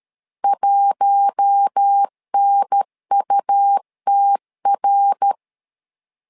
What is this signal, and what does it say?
Morse code, heard as a keyed tone, about 13 wpm, 800 Hz. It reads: 1NUTR
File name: LEZIONI / 4 Q1 12wpm 2028